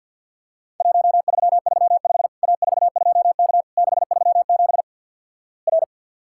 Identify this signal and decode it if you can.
Morse code, heard as a keyed tone, about 50 wpm, 700 Hz. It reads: 1335 A42K 637 R
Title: Worked All Europe DX Contest - QTC Exchange - 50wpm